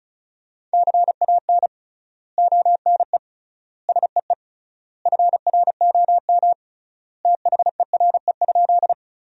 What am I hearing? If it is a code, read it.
Morse code, heard as a keyed tone, about 34 wpm, 700 Hz. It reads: CAN ONE SEE FROM THERE?
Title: 3x Word-Spacing - Sentences from Top 100 Words - 35wpm